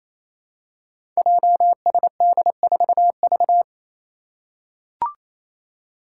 Morse code, heard as a keyed tone, about 28 wpm, 700 Hz. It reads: JSD4V
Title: Five Challenging Letter-Number Combinations 28wpm